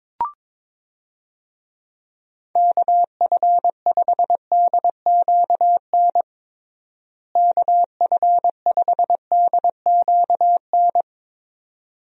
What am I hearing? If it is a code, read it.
Morse code, heard as a keyed tone, about 22 wpm, 700 Hz. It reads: KF5DQN KF5DQN
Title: USA Call Signs - 6 Characters - Repeated 2x - 22wpm